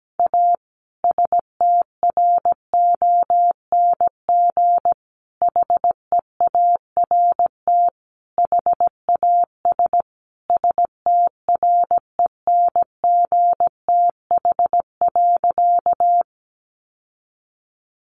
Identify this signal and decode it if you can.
Morse code, heard as a keyed tone, about 17 wpm, 700 Hz. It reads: A STRONG HEART HAS STRENGTH.
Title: Sentences from Top 1300 Words 17wpm